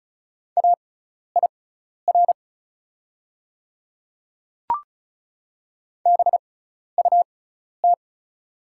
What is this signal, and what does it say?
Morse code, heard as a keyed tone, about 36 wpm, spaced as for 12 wpm, 690 Hz. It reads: AIR BUT
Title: 6x Character Spacing - Top 300 Words - 35wpm